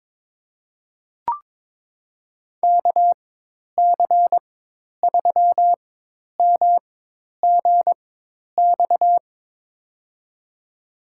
Morse code, heard as a keyed tone, about 22 wpm, spaced as for 10 wpm, 700 Hz. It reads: KC3MGX
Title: USA Call Signs - 4x Character Spacing - 22wpm